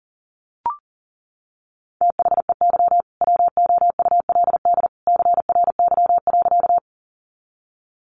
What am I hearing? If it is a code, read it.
Morse code, heard as a keyed tone, about 40 wpm, 700 Hz. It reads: THEY WOULD CRY.
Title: Sentences from Top 500 Words 40wpm